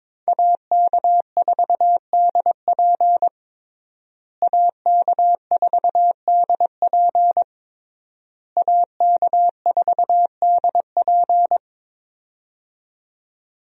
Morse code, heard as a keyed tone, about 22 wpm, 700 Hz. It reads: AK4DP AK4DP AK4DP
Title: USA Call Signs - 5 Characters - Repeated 3x - 22wpm